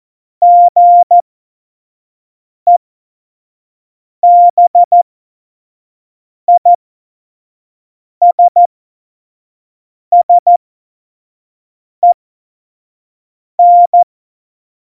Morse code, heard as a keyed tone, about 14 wpm, spaced as for 5 wpm, 700 Hz. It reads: GEBISSEN
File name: Morsekurs / Witze / Witz 27 Juni 17, Tempo 25 F70